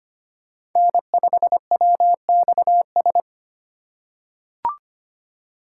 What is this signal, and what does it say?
Morse code, heard as a keyed tone, about 25 wpm, 700 Hz. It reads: N5WXS
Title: USA Call Signs - 5 Characters - 3x Character Spacing - 25wpm